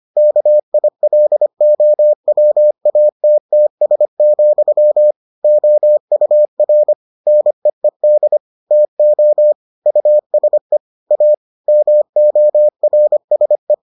Morse code, heard as a keyed tone, about 25 wpm, 600 Hz. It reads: KILOWATTS, OUR NEED TO USE A MORSE